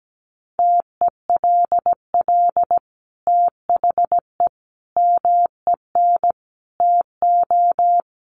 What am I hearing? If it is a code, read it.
Morse code, heard as a keyed tone, about 17 wpm, 700 Hz. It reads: TELL THE MEN TO
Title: Sentences from Top 1400 Words 17wpm